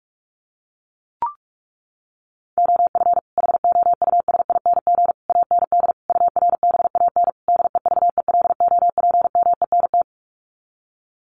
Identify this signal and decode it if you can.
Morse code, heard as a keyed tone, about 44 wpm, 700 Hz. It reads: OF HOUSING AND URBAN DEVELOPMENT
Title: Sets of 5 Words Encore2 45wpm